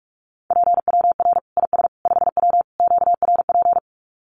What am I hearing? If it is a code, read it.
Morse code, heard as a keyed tone, about 44 wpm, 700 Hz. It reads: PWR IS 5W QRP
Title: Common QSO Phrases 45wpm